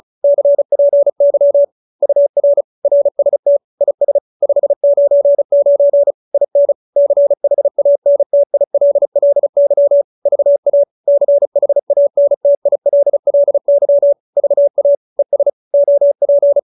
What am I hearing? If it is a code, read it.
Morse code, heard as a keyed tone, about 35 wpm, 570 Hz. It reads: CPY UR RST IS 599 IN CHANTILLY VA CHANTILLY VA ES OP